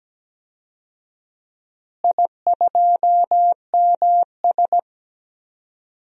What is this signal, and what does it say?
Morse code, heard as a keyed tone, about 17 wpm, 700 Hz. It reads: I2MS
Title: DXCC-100 Call Signs - 4 Characters - Repeated 3x - 17wpm